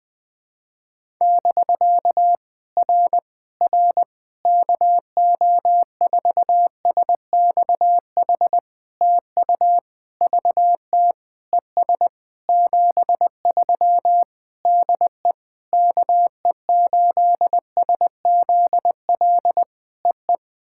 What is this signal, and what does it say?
Morse code, heard as a keyed tone, about 21 wpm, 700 Hz. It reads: <BK> R R KO4SXH TU VT ES 73 DE KE8SZL EE
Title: POTA QSO Practice - 20wpm